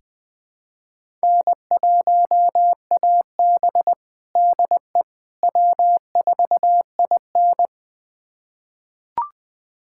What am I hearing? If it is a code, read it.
Morse code, heard as a keyed tone, about 20 wpm, 700 Hz. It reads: N1AB DE W4IN
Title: USA Callsign DE Callsign - 4 Characters - 20wpm